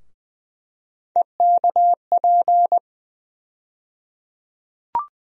Morse code, heard as a keyed tone, about 20 wpm, 700 Hz. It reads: EKP
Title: USA Call Sign Postfixes - 3 Letters 20wpm